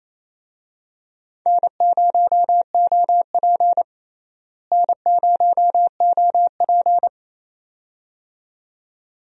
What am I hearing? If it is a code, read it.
Morse code, heard as a keyed tone, about 28 wpm, 700 Hz. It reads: N0OP N0OP
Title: USA Call Signs - 4 Characters - Repeated 2x - 28wpm